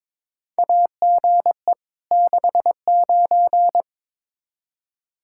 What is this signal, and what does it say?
Morse code, heard as a keyed tone, about 21 wpm, 700 Hz. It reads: AGE 69
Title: Practice QSO - Exchange 3 - More Info - 22wpm